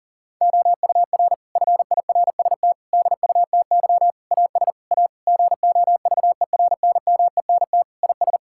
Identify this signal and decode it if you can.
Morse code, heard as a keyed tone, about 37 wpm, 700 Hz. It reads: OUR FIRST DUTY AS A GOVERNMENT IS
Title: Sentences from Top 1600 Words 40wpm